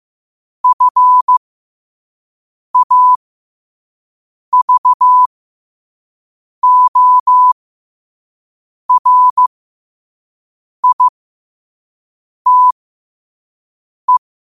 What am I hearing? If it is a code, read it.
Morse code, heard as a keyed tone, about 15 wpm, spaced as for 5 wpm, 1000 Hz. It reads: FAVORITE
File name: qotdfiles / 5wpm